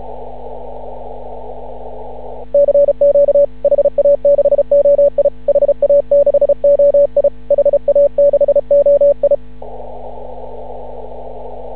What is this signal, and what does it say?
Morse code, heard as a keyed tone, about 35 wpm, 560 Hz. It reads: CQ HA6OI HA6OI HA6OI
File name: CW / ha6oi 2